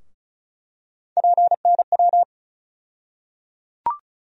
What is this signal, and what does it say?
Morse code, heard as a keyed tone, about 34 wpm, 700 Hz. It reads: PNW
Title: USA Call Sign Postfixes - 3 Letters 35wpm